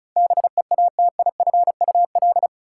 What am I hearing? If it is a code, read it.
Morse code, heard as a keyed tone, about 34 wpm, 700 Hz. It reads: BEATIFUL
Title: Sentences from Top 1300 Words - New Word Drill Alt - 35wpm